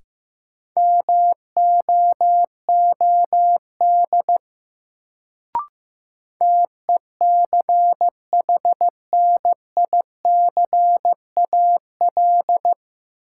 Morse code, heard as a keyed tone, about 15 wpm, 700 Hz. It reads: MOOD TECHNICAL
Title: Sentences from Top 1100 Words - New Word Drill Alt - 15wpm